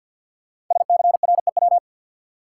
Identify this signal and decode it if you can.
Morse code, heard as a keyed tone, about 51 wpm, 700 Hz. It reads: SCREW
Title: Sentences from Top 1600 Words - New Word Drill - 50wpm